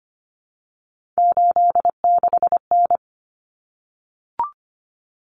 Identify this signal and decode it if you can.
Morse code, heard as a keyed tone, about 25 wpm, 700 Hz. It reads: 86N
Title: Common Cut Numbers - 3-Digit 25wpm